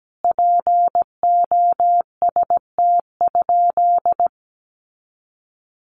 Morse code, heard as a keyed tone, about 17 wpm, 700 Hz. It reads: POST?
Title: Sentences from Top 1000 Words 17wpm